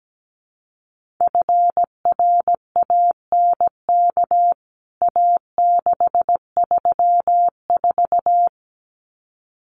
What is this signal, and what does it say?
Morse code, heard as a keyed tone, about 17 wpm, 700 Hz. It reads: FRANK A634
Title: CWT Exchange 17wpm